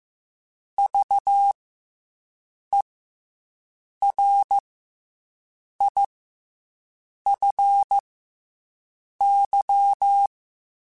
Morse code, heard as a keyed tone, about 15 wpm, 800 Hz. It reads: V E R I F Y
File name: Special INGs / Special INGs-mp3s / SlowToFaster ing @13